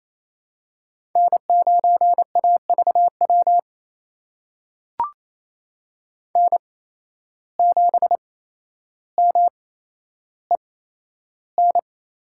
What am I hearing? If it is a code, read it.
Morse code, heard as a keyed tone, about 27 wpm, spaced as for 12 wpm, 700 Hz. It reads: N9AVW N7MEN
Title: USA Call Signs - 5 Characters - 8x Character Spacing - 28wpm